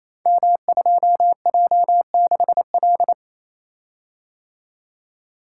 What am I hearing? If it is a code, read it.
Morse code, heard as a keyed tone, about 28 wpm, 700 Hz. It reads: M2J6L
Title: Code Groups - 5 Letter:Numbers - 28wpm